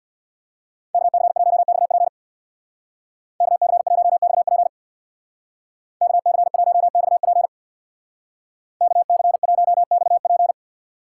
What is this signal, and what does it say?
Morse code, heard as a keyed tone, about 59 wpm, 700 Hz. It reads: KC1XP KC1XP KC1XP KC1XP
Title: Speed Racing - USA 2x2 Call Signs 50wpm